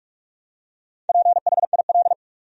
Morse code, heard as a keyed tone, about 46 wpm, 700 Hz. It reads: PHIL
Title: USA Common Names - Max 4 Letters - 45wpm